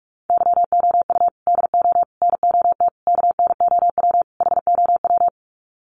Morse code, heard as a keyed tone, about 44 wpm, 700 Hz. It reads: YOU DO NOT KNOW HOW
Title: Sets of 5 Words Encore 45wpm